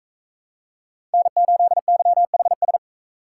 Morse code, heard as a keyed tone, about 44 wpm, 700 Hz. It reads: N8YHS